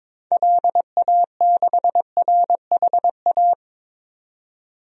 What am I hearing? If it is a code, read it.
Morse code, heard as a keyed tone, about 22 wpm, 700 Hz. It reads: LA6RHA
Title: DXCC-100 Call Signs - 6 Characters - 22wpm